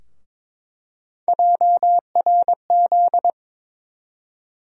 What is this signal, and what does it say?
Morse code, heard as a keyed tone, about 22 wpm, 700 Hz. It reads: JRZ
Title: USA Call Sign Postfixes - 3 Letters 22wpm